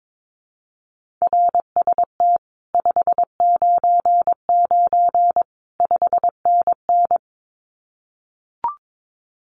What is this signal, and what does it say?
Morse code, heard as a keyed tone, about 22 wpm, 700 Hz. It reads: RST 599 5NN
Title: Common QSO Phrases Encore 22wpm